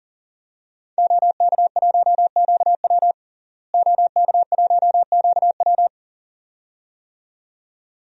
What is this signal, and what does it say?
Morse code, heard as a keyed tone, about 40 wpm, 700 Hz. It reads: OK1QW OK1QW
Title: DXCC-100 Call Signs - 5 Characters - Repeated 2x - 40wpm